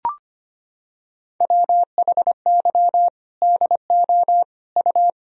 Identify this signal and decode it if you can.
Morse code, heard as a keyed tone, about 25 wpm, 700 Hz. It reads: WHY DO U
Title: Sentences from Top 400 Words and Abbreviations 25wpm